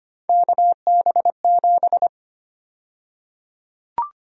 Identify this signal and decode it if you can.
Morse code, heard as a keyed tone, about 25 wpm, 700 Hz. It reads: KB7
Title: USA Call Sign Prefixes - Repeated 2x - 25wpm